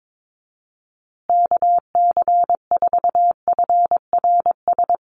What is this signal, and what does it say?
Morse code, heard as a keyed tone, about 22 wpm, 700 Hz. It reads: KC4FRS
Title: USA Call Signs Repeated 3x 22wpm